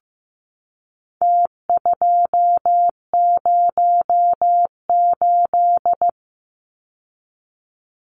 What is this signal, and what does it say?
Morse code, heard as a keyed tone, about 15 wpm, 700 Hz. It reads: T208
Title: Numbers - Contest Style - 4-Digit 15wpm